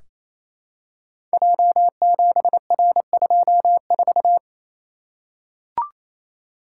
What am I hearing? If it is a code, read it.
Morse code, heard as a keyed tone, about 28 wpm, 700 Hz. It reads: J7R24
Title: Code Groups - 5 Letter:Numbers - 28wpm